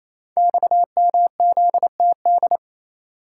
Morse code, heard as a keyed tone, about 27 wpm, 700 Hz. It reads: XMZTD
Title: Code Groups - 5 Letter:Numbers - 28wpm